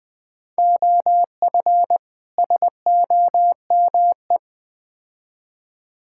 Morse code, heard as a keyed tone, about 20 wpm, 700 Hz. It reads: OF SOME
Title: Build A Phrase - 4-Word Phrases - 20wpm